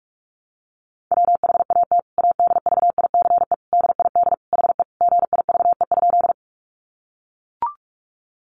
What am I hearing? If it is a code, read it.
Morse code, heard as a keyed tone, about 44 wpm, 700 Hz. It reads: WHAT ADVICE DID HE GIVE?
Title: Sentences from Top 1100 Words 45wpm